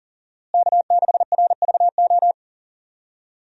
Morse code, heard as a keyed tone, about 40 wpm, 700 Hz. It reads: K6RVO
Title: USA Call Signs - 5 Characters - Repeated 3x - 40wpm